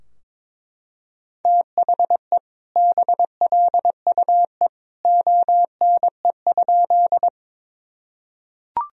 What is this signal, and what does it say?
Morse code, heard as a keyed tone, about 21 wpm, 700 Hz. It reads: THE BLUE ONE?
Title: Sentences from Top 1000 Words - Review Part 3 - 22wpm